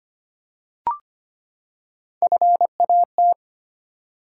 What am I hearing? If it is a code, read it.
Morse code, heard as a keyed tone, about 25 wpm, 700 Hz. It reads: FAT
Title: Three Letter Words - 25wpm